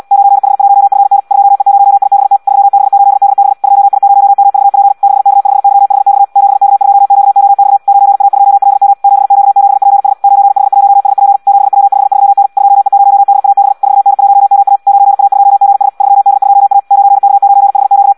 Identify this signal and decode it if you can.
Morse code, heard as a keyed tone, about 74 wpm, 800 Hz. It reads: 9H1FN 2E0IBT 3B8DB 2I0MVP 6K5YHX 7K2QOX 9A2LM 7Q7PS 9H1SP 7W53N 2E0DAB 4N1MA 9A2GS 3D2A 9K9HQ
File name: Audio practica CW / qrq-tx-74 2